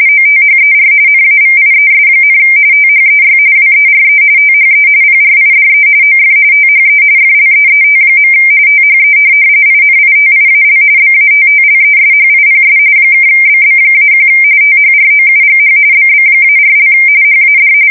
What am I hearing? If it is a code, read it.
RTTY, about 45.45 baud. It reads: OTHER NAVIES, THEY WERE THE PRIDE OF THE FLEET. THE INITIAL BATTLESHIPS (BB-1 THROUGH BB-4) HAD
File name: fsk170 / USS-New-Jersey